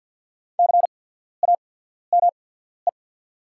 Morse code, heard as a keyed tone, about 50 wpm, spaced as for 14 wpm, 690 Hz. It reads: CAME
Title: 8x Character Spacing - Top 200 Words - 50wpm